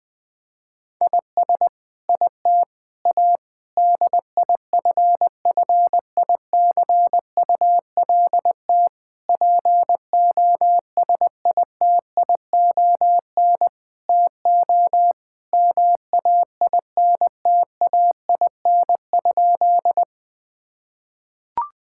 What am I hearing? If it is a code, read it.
Morse code, heard as a keyed tone, about 20 wpm, 700 Hz. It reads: IS IT A DIFFICULT POSITION TO MAINTAIN?
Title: Sentences from Top 1000 Words - Review Part 5 - 20wpm